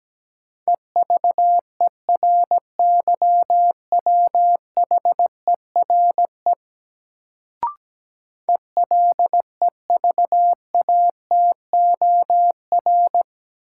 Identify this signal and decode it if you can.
Morse code, heard as a keyed tone, about 17 wpm, 700 Hz. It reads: EVERYWHERE ELEVATOR
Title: Sentences from Top 1600 Words - New Word Drill - 17wpm